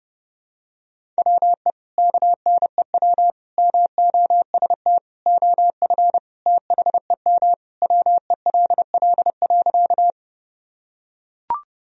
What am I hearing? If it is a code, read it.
Morse code, heard as a keyed tone, about 30 wpm, 700 Hz. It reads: WE KNEW MOST OF THEM WELL.
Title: Sentences from Top 1000 Words - Review Part 1 - 30wpm